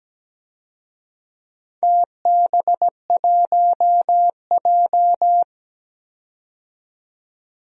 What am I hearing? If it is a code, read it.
Morse code, heard as a keyed tone, about 17 wpm, 700 Hz. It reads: TB1J
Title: DXCC-100 Call Signs - 4 Characters - 17wpm